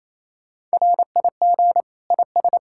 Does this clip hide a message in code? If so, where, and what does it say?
Yes: Morse code from 0.7 to 2.6 s, heard as a keyed tone, about 35 wpm, 700 Hz. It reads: RIG IS